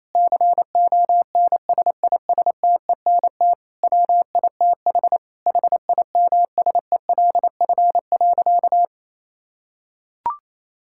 Morse code, heard as a keyed tone, about 28 wpm, 700 Hz. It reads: CONSISTENT WITH HIMSELF.
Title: Sentences from Top 1600 Words 28wpm